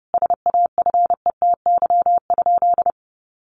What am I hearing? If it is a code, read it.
Morse code, heard as a keyed tone, about 30 wpm, 700 Hz. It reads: SAFETY?
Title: Sentences from Top 700 Words 30wpm